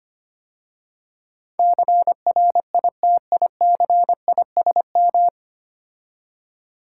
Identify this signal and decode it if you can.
Morse code, heard as a keyed tone, about 26 wpm, 700 Hz. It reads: CRITICISM
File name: 9 Letter Words - First 3 Letters Given - 25wpm